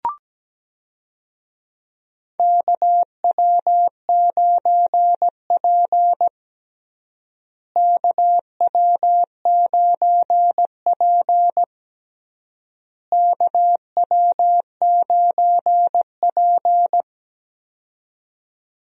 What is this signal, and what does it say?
Morse code, heard as a keyed tone, about 17 wpm, 700 Hz. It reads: KW9P KW9P KW9P
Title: USA Call Signs - 4 Characters - Repeated 3x - 17wpm